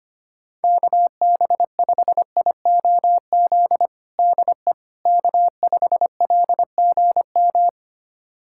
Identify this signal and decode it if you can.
Morse code, heard as a keyed tone, about 25 wpm, 700 Hz. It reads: KB5IOZ DE K5LGM
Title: USA Callsign DE Callsign 25wpm